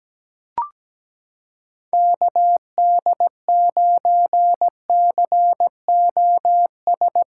Morse code, heard as a keyed tone, about 17 wpm, 700 Hz. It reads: KD9COS
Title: USA Callsign DE Callsign - 6 Characters - 17wpm